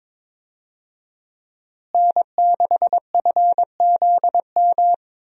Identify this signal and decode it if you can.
Morse code, heard as a keyed tone, about 22 wpm, 700 Hz. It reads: N6FZM
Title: USA Call Signs - 5 Characters - Repeated 3x - 22wpm